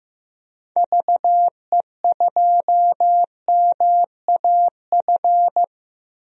Morse code, heard as a keyed tone, about 15 wpm, 700 Hz. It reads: VE2MAF
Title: DXCC-100 Call Signs - 6 Characters - Repeated 3x - 15wpm